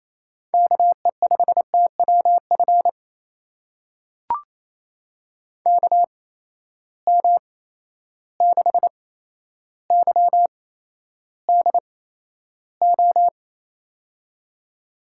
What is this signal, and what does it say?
Morse code, heard as a keyed tone, about 28 wpm, spaced as for 12 wpm, 700 Hz. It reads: KE5TWF KM6YDO
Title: USA Call Signs - 6 Characters - 8x Character Spacing - 28wpm